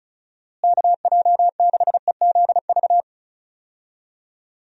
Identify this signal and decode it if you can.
Morse code, heard as a keyed tone, about 34 wpm, 700 Hz. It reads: KJ6EZV
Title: USA Call Signs - 6 Characters - Repeated 2x - 35wpm